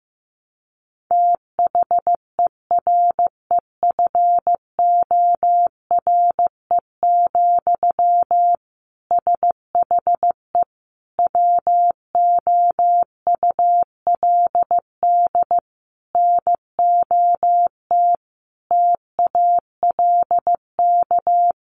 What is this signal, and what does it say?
Morse code, heard as a keyed tone, about 15 wpm, 700 Hz. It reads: THEREFORE, SHE WOULD NOT TALK
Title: Sentences from Top 800 Words 15wpm